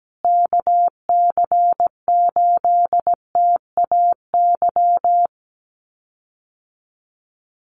Morse code, heard as a keyed tone, about 17 wpm, 700 Hz. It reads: KC8TAY
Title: USA Call Signs Repeated 2x 17wpm